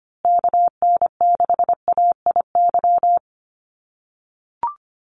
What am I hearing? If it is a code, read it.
Morse code, heard as a keyed tone, about 25 wpm, 700 Hz. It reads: KN6AIY
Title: USA Call Signs Repeated 2x 25wpm